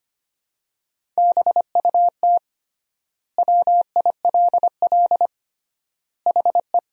Morse code, heard as a keyed tone, about 25 wpm, 700 Hz. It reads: BUT WILL HE
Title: 3x Word-Spacing - Sentences from Top 100 Words - 25wpm